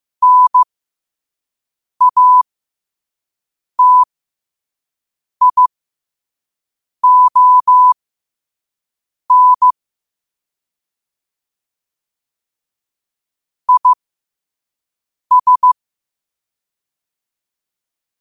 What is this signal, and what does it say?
Morse code, heard as a keyed tone, about 15 wpm, spaced as for 5 wpm, 1000 Hz. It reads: NATION IS
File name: qotdfiles / 5wpm